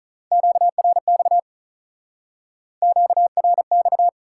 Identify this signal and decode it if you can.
Morse code, heard as a keyed tone, about 46 wpm, 700 Hz. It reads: QRX QRX